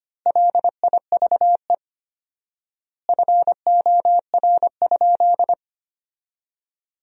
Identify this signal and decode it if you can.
Morse code, heard as a keyed tone, about 25 wpm, 700 Hz. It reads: LIVE FOR?
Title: 4x Word-Spacing - Sentences from Top 200 Words - 25wpm